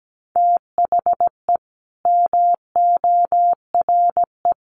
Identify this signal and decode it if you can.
Morse code, heard as a keyed tone, about 17 wpm, 700 Hz. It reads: THE MORE
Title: Sentences from Top 1300 Words 17wpm